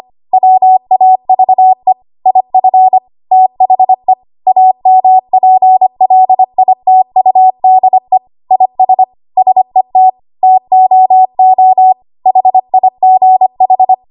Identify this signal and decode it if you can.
Morse code, heard as a keyed tone, about 25 wpm, 750 Hz. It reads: WAVE IF THE AMPLITUDE IS SET TOO HIGH
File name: Archive / 25 WPM / 25WPM